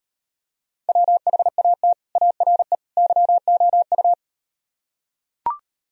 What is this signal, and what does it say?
Morse code, heard as a keyed tone, about 38 wpm, 700 Hz. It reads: WHAT ARE YOU